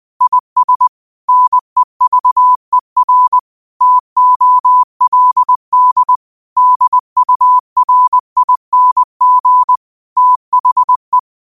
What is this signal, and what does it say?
Morse code, heard as a keyed tone, about 20 wpm, 1000 Hz. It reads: IS NEVER TOLD DURING THE
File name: qotdfiles / 20wpm